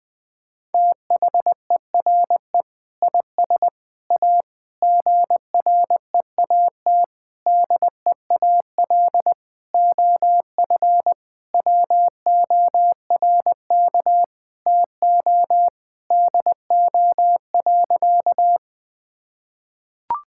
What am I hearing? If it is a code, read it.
Morse code, heard as a keyed tone, about 20 wpm, 700 Hz. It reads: THERE IS A GREAT DEAL OF WORK TO DO.
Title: Sentences with 6-Word Phrases - Spoken Beginning - 20wpm